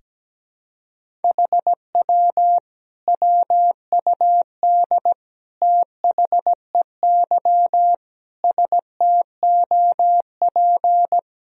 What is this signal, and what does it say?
Morse code, heard as a keyed tone, about 17 wpm, 700 Hz. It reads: HW WUD THEY STOP
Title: Sentences from Top 300 Words and Abbreviations 17wpm